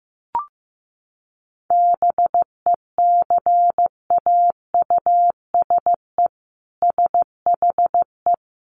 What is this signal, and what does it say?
Morse code, heard as a keyed tone, about 15 wpm, 700 Hz. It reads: BECAUSE SHE
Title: Sentences from Top 1100 Words 15wpm